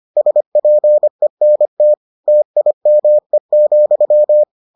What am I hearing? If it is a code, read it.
Morse code, heard as a keyed tone, about 25 wpm, 600 Hz. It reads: SPENT TIME,